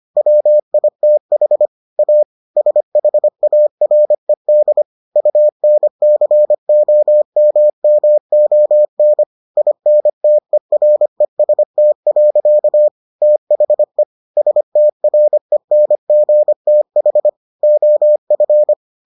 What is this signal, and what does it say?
Morse code, heard as a keyed tone, about 25 wpm, 600 Hz. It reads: WITH A SHARED UNCOMMON INTEREST. THE STRENGTH OF